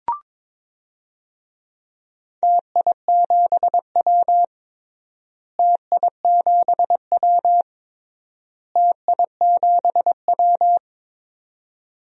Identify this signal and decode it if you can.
Morse code, heard as a keyed tone, about 22 wpm, 700 Hz. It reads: TI7W TI7W TI7W
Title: DXCC-100 Call Signs - 4 Characters - Repeated 3x - 22wpm